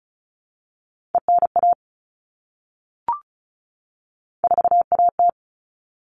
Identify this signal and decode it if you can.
Morse code, heard as a keyed tone, about 34 wpm, 700 Hz. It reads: ENA 4AT
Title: All Cut Numbers - 3-Digit 35wpm